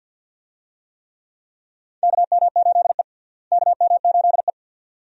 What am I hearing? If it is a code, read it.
Morse code, heard as a keyed tone, about 52 wpm, 700 Hz. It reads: KM8E KM8E
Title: USA Call Signs - 4 Characters - Repeated 2x - 50wpm